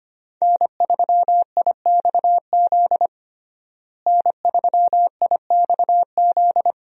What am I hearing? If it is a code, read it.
Morse code, heard as a keyed tone, about 25 wpm, 700 Hz. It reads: N3IXZ N3IXZ
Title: USA Call Signs - 5 Characters - Repeated 2x - 25wpm